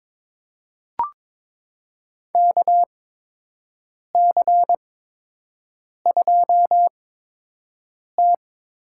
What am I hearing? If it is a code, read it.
Morse code, heard as a keyed tone, about 22 wpm, spaced as for 6 wpm, 700 Hz. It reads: KC2T
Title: USA Call Signs - 6 Characters - 8x Character Spacing - 22wpm